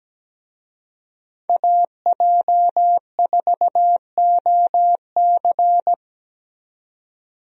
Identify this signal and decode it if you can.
Morse code, heard as a keyed tone, about 17 wpm, 700 Hz. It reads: AJ4OC
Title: USA Call Signs - 5 Characters - 17wpm